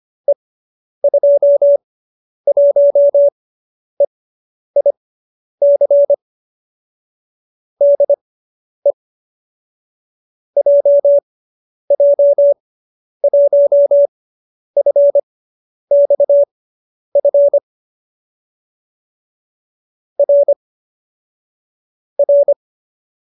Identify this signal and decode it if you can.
Morse code, heard as a keyed tone, about 25 wpm, spaced as for 10 wpm, 570 Hz. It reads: E21EIC DE JJ1FXF R R